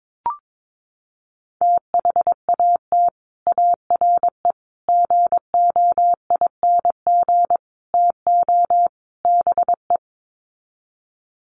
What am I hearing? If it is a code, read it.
Morse code, heard as a keyed tone, about 22 wpm, 700 Hz. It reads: THAT ARE GOING TO BE
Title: Sets of 5 Words Encore2 22wpm